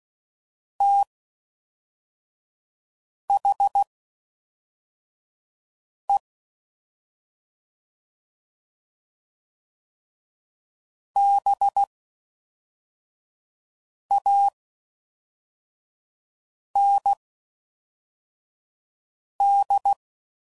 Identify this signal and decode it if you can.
Morse code, heard as a keyed tone, about 16 wpm, spaced as for 4 wpm, 800 Hz. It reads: THE BAND